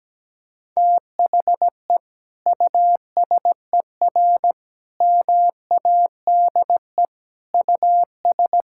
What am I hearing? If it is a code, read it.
Morse code, heard as a keyed tone, about 17 wpm, 700 Hz. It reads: THE USER MADE US
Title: Sentences from Top 1000 Words - Review Part 5 - 17wpm